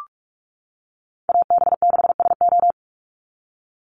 Morse code, heard as a keyed tone, about 46 wpm, 690 Hz. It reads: AB6SO
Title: NA SOTA Chaser Callsigns 45wpm